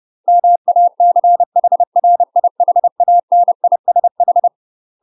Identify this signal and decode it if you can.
Morse code, heard as a keyed tone, about 30 wpm, 700 Hz. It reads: MACHRIHANISH
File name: UK MASTER QSOs TRAINING / 30WPM / UK MASTER QSOs 0021